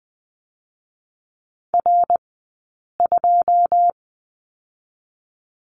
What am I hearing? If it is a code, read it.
Morse code, heard as a keyed tone, about 20 wpm, 700 Hz. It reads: R 2
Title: 106 Lesson 18d - Letter m - Two Characters 20wpm